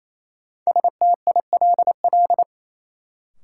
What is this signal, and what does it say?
Morse code, heard as a keyed tone, about 28 wpm, 700 Hz. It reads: STILL
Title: Sentences from Top 300 Words - New Word Drill - 28wpm